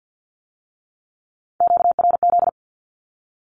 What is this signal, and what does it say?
Morse code, heard as a keyed tone, about 50 wpm, 700 Hz. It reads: QRZ
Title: Q-Codes with Spoken Meaning - 50wpm